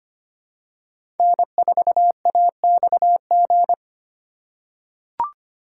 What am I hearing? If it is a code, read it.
Morse code, heard as a keyed tone, about 25 wpm, 700 Hz. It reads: N4AXG
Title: USA Call Signs - 5 Characters - 3x Character Spacing - 25wpm